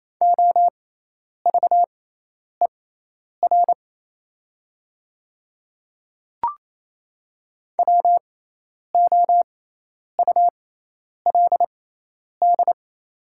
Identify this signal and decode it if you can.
Morse code, heard as a keyed tone, about 28 wpm, spaced as for 10 wpm, 700 Hz. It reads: OVER WOULD
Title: 6x Character Spacing - Top 100 Words - 28wpm